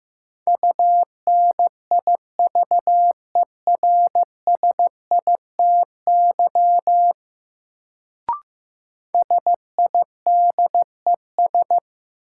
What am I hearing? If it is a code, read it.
Morse code, heard as a keyed tone, about 15 wpm, 700 Hz. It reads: UNIVERSITY SIDES
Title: Sets of 4 Words - New Word Drill - 15wpm